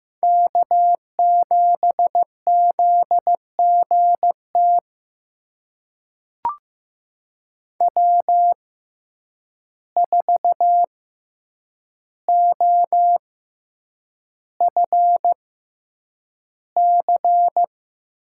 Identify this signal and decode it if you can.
Morse code, heard as a keyed tone, about 15 wpm, spaced as for 8 wpm, 700 Hz. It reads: K7ZGT W4OFC
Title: USA Call Signs - 5 Characters - 6x Character Spacing - 15wpm